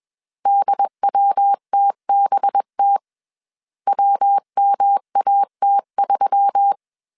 Morse code, heard as a keyed tone, about 21 wpm, 800 Hz. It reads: DWTBT WMAT3
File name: CW / DET / 1 6 1500C S B 20wpm 2048